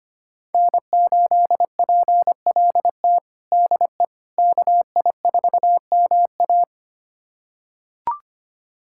Morse code, heard as a keyed tone, about 25 wpm, 700 Hz. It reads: N8PLT DE KI4MA
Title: USA Callsign DE Callsign - 5 Characters - 25wpm